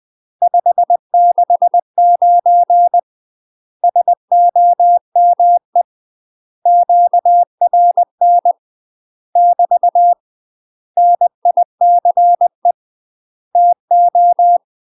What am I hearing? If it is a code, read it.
Morse code, heard as a keyed tone, about 20 wpm, 700 Hz. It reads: 569 SOME QRN <BT> NICE TO
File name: MASTER QSOs TRAINING/20WPM